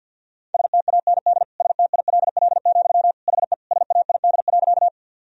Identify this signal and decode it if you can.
Morse code, heard as a keyed tone, about 51 wpm, 700 Hz. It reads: STAND STILL, HE SAID.
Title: Sentences from Top 1000 Words - Review Part 1 - 50wpm